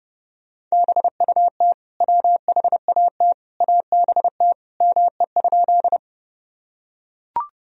Morse code, heard as a keyed tone, about 32 wpm, 700 Hz. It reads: BUT WHAT ABT ME?
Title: Sentences from Top 200 Words and Abbreviations 30wpm